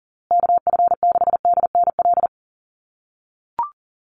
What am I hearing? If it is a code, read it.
Morse code, heard as a keyed tone, about 40 wpm, 700 Hz. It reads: KF6DNL
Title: USA Call Signs Repeated 3x 40wpm